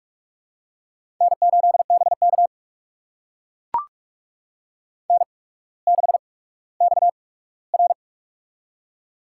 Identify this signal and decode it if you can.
Morse code, heard as a keyed tone, about 44 wpm, spaced as for 19 wpm, 700 Hz. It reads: N8BK N6XR
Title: USA Call Signs - 4 Characters - 8x Character Spacing - 45wpm